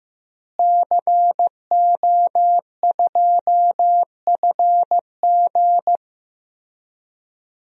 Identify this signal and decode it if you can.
Morse code, heard as a keyed tone, about 15 wpm, 700 Hz. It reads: CO2FG
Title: DXCC-100 Call Signs - 5 Characters - Repeated 2x - 15wpm